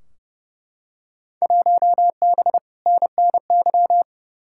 Morse code, heard as a keyed tone, about 32 wpm, 700 Hz. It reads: 1B NNY
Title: Field Day Exchanges - 30wpm